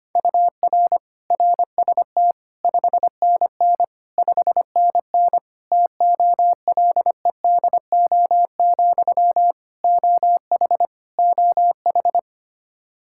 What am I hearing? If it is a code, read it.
Morse code, heard as a keyed tone, about 25 wpm, 700 Hz. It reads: UR RST 5NN 5NN TOLEDO, OH OH
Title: Practice QSO - Exchange 1 - Beginning - 25wpm